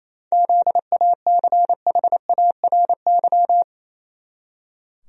Morse code, heard as a keyed tone, about 27 wpm, 700 Hz. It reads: ZACHARY
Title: USA Common Names - 28wpm